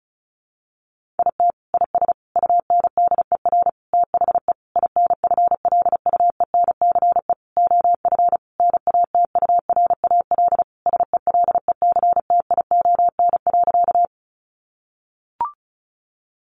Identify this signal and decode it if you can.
Morse code, heard as a keyed tone, about 34 wpm, 700 Hz. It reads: IT IS UNDER THE INFLUENCE OF NATURAL SELECTION.
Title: Sentences from Top 1400 Words 35wpm